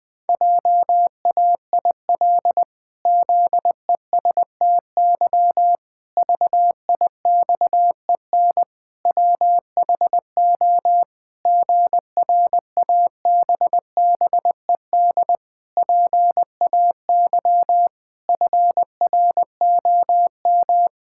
Morse code, heard as a keyed tone, about 20 wpm, 700 Hz. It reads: JAIL ZESTY VIXEN WHO GRABBED PAY FROM
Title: Pangram Challenge v2 - 20wpm